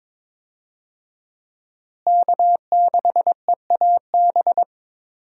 Morse code, heard as a keyed tone, about 21 wpm, 700 Hz. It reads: K6EAB
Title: USA Call Signs - 5 Characters - Repeated 2x - 22wpm